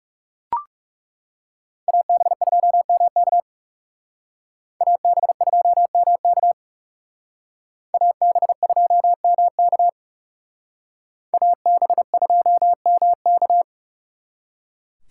Morse code, heard as a keyed tone, about 38 wpm, 700 Hz. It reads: AB2MK AB2MK AB2MK AB2MK